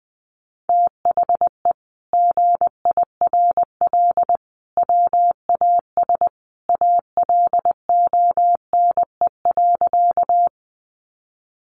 Morse code, heard as a keyed tone, about 20 wpm, 700 Hz. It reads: THE GIRL WAS ALONE.
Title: Sentences from Top 1000 Words 20wpm